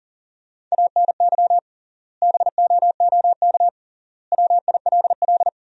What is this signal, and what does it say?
Morse code, heard as a keyed tone, about 40 wpm, 700 Hz. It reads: ANY BOOK WILL
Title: 3x Word-Spacing - Sentences from Top 300 Words - 40wpm